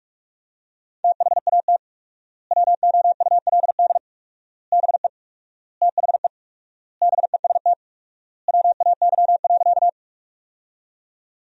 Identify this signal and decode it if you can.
Morse code, heard as a keyed tone, about 43 wpm, 700 Hz. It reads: THAT WOULD BE THE BEST WAY.
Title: 4x Word-Spacing - Sentences from Top 400 Words - 45wpm